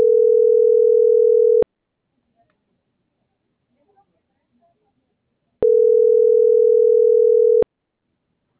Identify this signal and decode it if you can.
Ringback tone (North America), 440 + 480 Hz.
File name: CENTRAL TERRESTRE / out karis - .78980